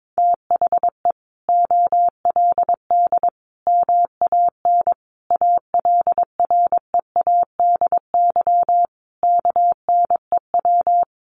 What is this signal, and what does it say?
Morse code, heard as a keyed tone, about 22 wpm, 700 Hz. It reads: THE OLD MAN ALREADY KNEW
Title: Sentences from Top 700 Words 22wpm